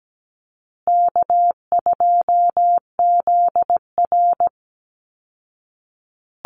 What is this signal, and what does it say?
Morse code, heard as a keyed tone, about 17 wpm, 700 Hz. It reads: K2ZR
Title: Call Signs Repeated 2x - 17wpm